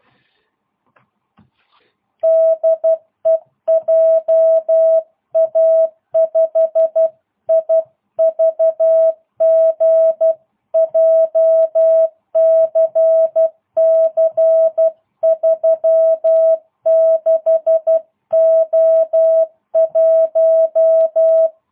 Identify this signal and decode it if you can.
Morse code, heard as a keyed tone, about 12 wpm, 650 Hz. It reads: DE JA5IVG JCC 36O1